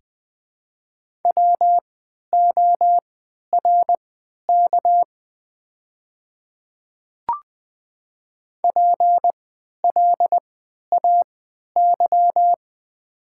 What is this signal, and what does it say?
Morse code, heard as a keyed tone, about 20 wpm, spaced as for 11 wpm, 700 Hz. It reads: WORK PLAY
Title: 3x Character Spacing - Top 300 Words - 20wpm